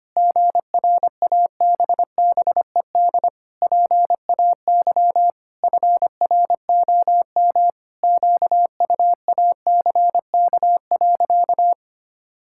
Morse code, heard as a keyed tone, about 25 wpm, 700 Hz. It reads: GRABBED PAY FROM QUACK.
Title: Pangram Challenge v2 - 25wpm